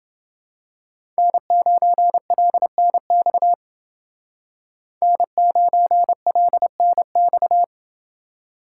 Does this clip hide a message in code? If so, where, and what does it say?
Yes: Morse code from 1.2 to 7.6 s, heard as a keyed tone, about 30 wpm, 700 Hz. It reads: N9LNX N9LNX